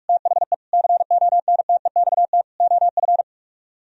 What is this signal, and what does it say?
Morse code, heard as a keyed tone, about 44 wpm, 700 Hz. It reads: THE CONTEXT OF